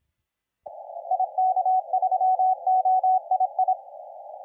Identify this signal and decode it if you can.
Morse code, heard as a keyed tone, about 26 wpm, 750 Hz, with radio noise. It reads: IK3OII
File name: la4c / sac-cw-2017 / CONTEST AUDIO 1100Z